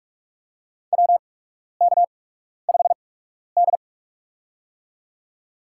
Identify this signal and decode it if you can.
Morse code, heard as a keyed tone, about 47 wpm, spaced as for 12 wpm, 690 Hz. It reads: WK5D
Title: USA Call Signs - 4 Characters - 8x Character Spacing - 45wpm